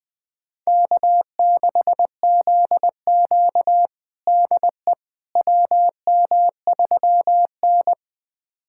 Morse code, heard as a keyed tone, about 20 wpm, 700 Hz. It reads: K6ZQ DE WM3N
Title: USA Callsign DE Callsign - 4 Characters - 20wpm